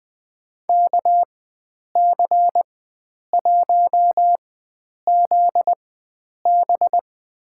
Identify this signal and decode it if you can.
Morse code, heard as a keyed tone, about 20 wpm, spaced as for 9 wpm, 700 Hz. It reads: KC1ZB
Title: USA Call Signs - 5 Characters - 4x Character Spacing - 20wpm